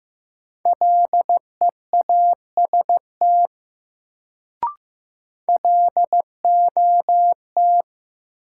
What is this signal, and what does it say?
Morse code, heard as a keyed tone, about 15 wpm, 700 Hz. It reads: LEAST LOT
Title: Sets of 2 Words - New Word Drill - 15wpm